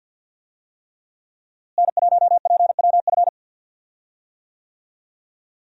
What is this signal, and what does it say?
Morse code, heard as a keyed tone, about 52 wpm, 700 Hz. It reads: N1PWF
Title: USA Call Signs - 5 Characters - Repeated 3x - 50wpm